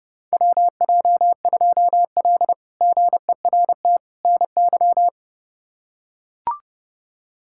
Morse code, heard as a keyed tone, about 30 wpm, 700 Hz. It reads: WJ2L GERT NY
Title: K1USN SST QSO Practice - 30wpm